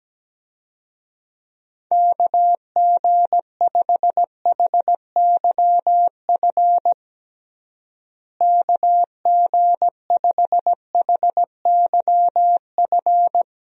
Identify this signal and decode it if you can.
Morse code, heard as a keyed tone, about 17 wpm, 700 Hz. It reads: KG5HYF KG5HYF
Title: USA Call Signs - 6 Characters - Repeated 3x - 17wpm